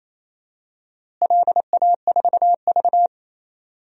Morse code, heard as a keyed tone, about 28 wpm, 700 Hz. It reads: LA4V
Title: DXCC-100 Call Signs - 4 Characters - Repeated 3x - 28wpm